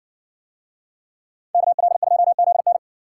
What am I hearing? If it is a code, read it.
Morse code, heard as a keyed tone, about 58 wpm, 700 Hz. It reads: KL2ZN